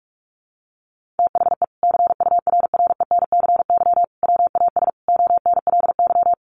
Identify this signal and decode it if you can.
Morse code, heard as a keyed tone, about 44 wpm, 700 Hz. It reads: THE CURRENCY WAS ONLY
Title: Sentences from Top 1500 Words 45wpm